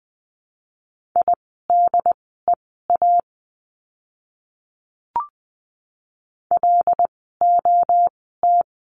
Morse code, heard as a keyed tone, about 20 wpm, spaced as for 14 wpm, 700 Hz. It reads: IDEA LOT
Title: 108 Lesson 18f - Letter m - Words 20wpm